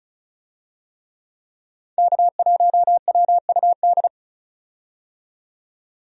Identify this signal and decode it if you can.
Morse code, heard as a keyed tone, about 34 wpm, 700 Hz. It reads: K1WUD
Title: USA Call Signs - 5 Characters - 35wpm